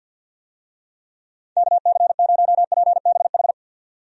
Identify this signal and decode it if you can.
Morse code, heard as a keyed tone, about 45 wpm, 700 Hz. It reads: KC0PBH
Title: USA Call Signs - 6 Characters - 50wpm